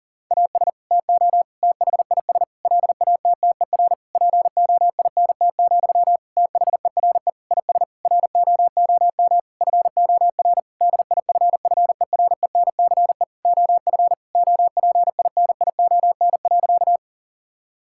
Morse code, heard as a keyed tone, about 40 wpm, 700 Hz. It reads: AS TO THIS LATTER POINT, THERE IS ROOM FOR DIFFERENCE OF OPINION.
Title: Sentences from Top 1600 Words 40wpm